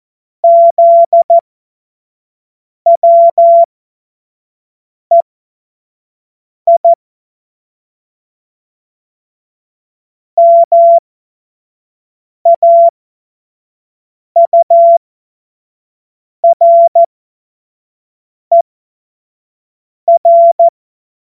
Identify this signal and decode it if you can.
Morse code, heard as a keyed tone, about 14 wpm, spaced as for 5 wpm, 680 Hz. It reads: ZWEI MAURER
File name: Morsekurs / Witze / Witz 02 Januar 2024 25 F70